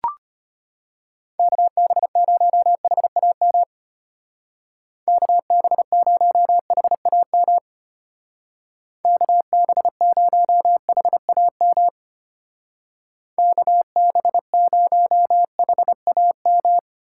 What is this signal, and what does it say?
Morse code, heard as a keyed tone, about 32 wpm, 700 Hz. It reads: KB0HAM KB0HAM KB0HAM KB0HAM